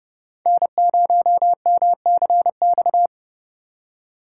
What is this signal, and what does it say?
Morse code, heard as a keyed tone, about 32 wpm, 700 Hz. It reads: N0MCX
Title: USA Call Signs - 5 Characters - 3x Character Spacing - 30wpm